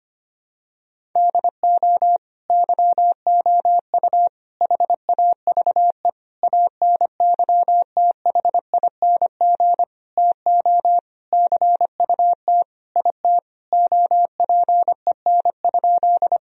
Morse code, heard as a keyed tone, about 25 wpm, 700 Hz. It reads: DO YOU HAVE ANYTHING TO CUT IT OPEN?
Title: Sentences from Top 1000 Words - Review Part 1 - 25wpm